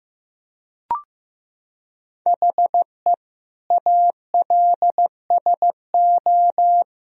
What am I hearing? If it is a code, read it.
Morse code, heard as a keyed tone, about 15 wpm, 700 Hz. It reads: HE ALSO
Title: Sentences from Top 1000 Words - Review Part 1 - 15wpm